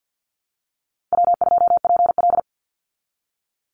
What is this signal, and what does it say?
Morse code, heard as a keyed tone, about 50 wpm, 700 Hz. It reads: W2PL
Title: Easy Call Signs Repeated 2x - 50wpm